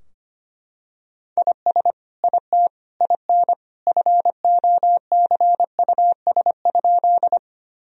Morse code, heard as a keyed tone, about 25 wpm, 700 Hz. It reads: IS IT IN FOCUS?
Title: Sentences from Top 1000 Words - Review Part 4 - 25wpm